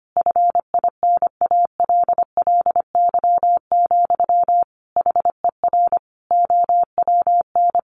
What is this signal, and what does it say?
Morse code, heard as a keyed tone, about 25 wpm, 700 Hz. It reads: FINALLY, HER OWN
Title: Sentences from Top 1100 Words 25wpm